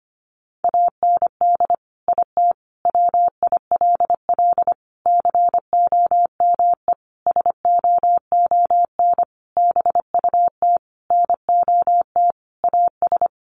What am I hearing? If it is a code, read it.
Morse code, heard as a keyed tone, about 25 wpm, 700 Hz. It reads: AND IT WILL COME SOON BUT NOT AS
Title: Sentences from Top 1100 Words 25wpm